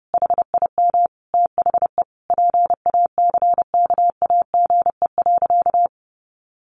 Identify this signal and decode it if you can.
Morse code, heard as a keyed tone, about 30 wpm, 700 Hz. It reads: HIM THE PACKAGE.
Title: Easy Phrases 30wpm